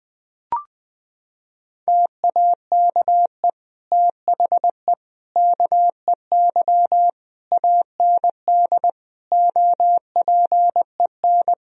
Morse code, heard as a keyed tone, about 21 wpm, 700 Hz. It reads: TAKE THE KEY AND OPEN
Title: Sentences from Top 1000 Words - Review Part 4 - 20wpm